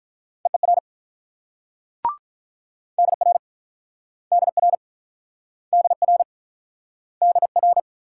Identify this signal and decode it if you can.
Morse code, heard as a keyed tone, about 46 wpm, 700 Hz. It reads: EER DR DR DR DR